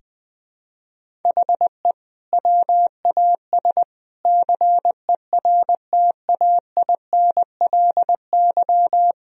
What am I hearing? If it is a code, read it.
Morse code, heard as a keyed tone, about 20 wpm, 700 Hz. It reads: HE WAS CERTAINLY
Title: Sentences from Top 1000 Words - Review Part 1 - 20wpm